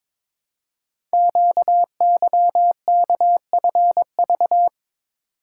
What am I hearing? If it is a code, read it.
Morse code, heard as a keyed tone, about 22 wpm, 700 Hz. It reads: QYKFV
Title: Code Groups - 5 Letters - 22wpm